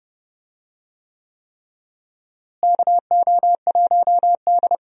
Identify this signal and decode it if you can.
Morse code, heard as a keyed tone, about 30 wpm, 700 Hz. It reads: KO1D
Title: USA Call Signs - 4 Characters - Repeated 3x - 30wpm